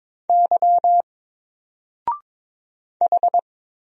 Morse code, heard as a keyed tone, about 22 wpm, 700 Hz. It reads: Y H
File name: Single-Letter-Number-22wpm